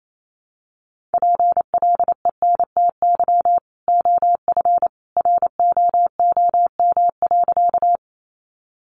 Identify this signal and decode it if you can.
Morse code, heard as a keyed tone, about 28 wpm, 700 Hz. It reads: PLENTY OF ROOM.
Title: Sentences from Top 1300 Words 28wpm